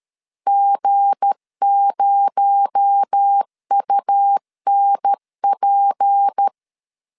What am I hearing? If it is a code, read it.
Morse code, heard as a keyed tone, about 13 wpm, 800 Hz. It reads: G0UNP
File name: LEZIONI / 7 GF9Z / 7 GF9 12wpm 2319